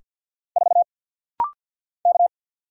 Morse code, heard as a keyed tone, about 50 wpm, 700 Hz. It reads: 4 K
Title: Single Letter-Number Alt - Rapid-Fire 50wpm